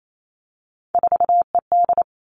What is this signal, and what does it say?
Morse code, heard as a keyed tone, about 28 wpm, 700 Hz. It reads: 4ED
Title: Three Challenging Letter-Number Combinations 28wpm